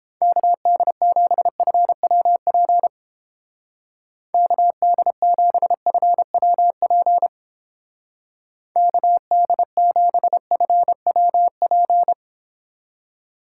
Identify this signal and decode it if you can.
Morse code, heard as a keyed tone, about 29 wpm, 700 Hz. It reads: KD7FWP KD7FWP KD7FWP